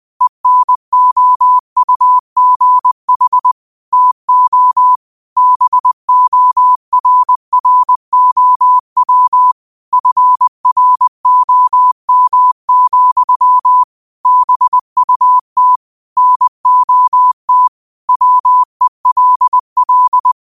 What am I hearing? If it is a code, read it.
Morse code, heard as a keyed tone, about 20 wpm, 1000 Hz. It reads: ENOUGH TO BORROW FROM, BUT NOT WELL